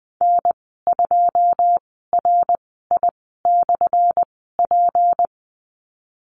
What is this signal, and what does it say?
Morse code, heard as a keyed tone, about 20 wpm, spaced as for 15 wpm, 700 Hz. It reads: N2RI/P
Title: 179 Lesson 29g - Slash - Call Signs 20wpm